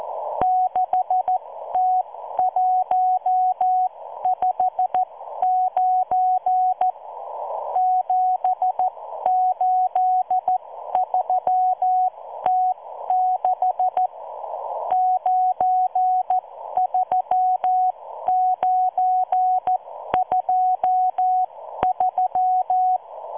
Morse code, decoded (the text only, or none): 6T159 783T6 93923